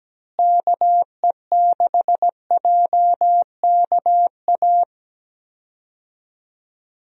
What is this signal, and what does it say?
Morse code, heard as a keyed tone, about 17 wpm, 700 Hz. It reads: KE6JKA
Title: USA Call Signs - 6 Characters - Repeated 2x - 17wpm